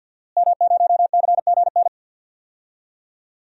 Morse code, heard as a keyed tone, about 51 wpm, 700 Hz. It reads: M0CGN
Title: DXCC-100 Call Signs - 5 Characters - Repeated 3x - 50wpm